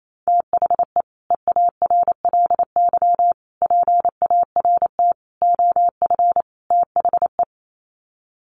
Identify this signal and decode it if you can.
Morse code, heard as a keyed tone, about 28 wpm, 700 Hz. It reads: THE EARLY PART OF THE
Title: Sets of 5 Words Encore2 28wpm